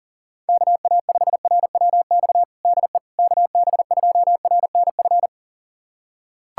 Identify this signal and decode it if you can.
Morse code, heard as a keyed tone, about 40 wpm, 690 Hz. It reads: KA5RWX DE KB2RNF
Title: USA Callsign DE Callsign - 6 Characters - 40wpm